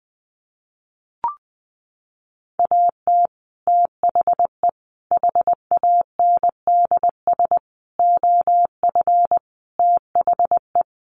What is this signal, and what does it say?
Morse code, heard as a keyed tone, about 19 wpm, 700 Hz. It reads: AT THE HANDS OF THE
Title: Sets of 5 Words Encore2 20wpm